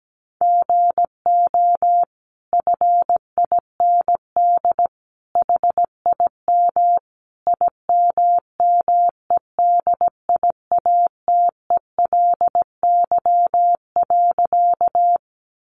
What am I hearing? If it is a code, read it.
Morse code, heard as a keyed tone, about 17 wpm, 700 Hz. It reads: GO FIND HIM IMMEDIATELY.
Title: Sentences from Top 1100 Words 17wpm